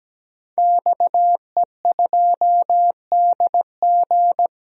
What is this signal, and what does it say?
Morse code, heard as a keyed tone, about 17 wpm, 700 Hz. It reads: XE2DG
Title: DXCC-100 Call Signs - 5 Characters - 17wpm